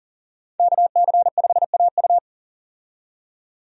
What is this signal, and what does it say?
Morse code, heard as a keyed tone, about 38 wpm, 700 Hz. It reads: KC5AU
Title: USA Call Signs - 5 Characters - 40wpm